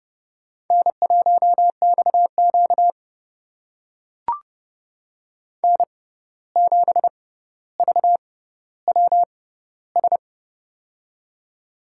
Morse code, heard as a keyed tone, about 29 wpm, spaced as for 14 wpm, 700 Hz. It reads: N1XQ N7VWS
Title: USA Call Signs - 6x Character Spacing - 30wpm